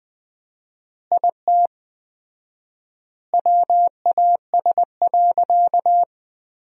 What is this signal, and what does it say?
Morse code, heard as a keyed tone, about 21 wpm, 700 Hz. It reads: IT WAS.
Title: 4x Word-Spacing - Sentences from Top 100 Words - 20wpm